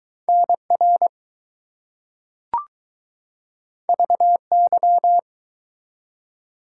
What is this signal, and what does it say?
Morse code, heard as a keyed tone, about 19 wpm, 700 Hz. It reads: NR VY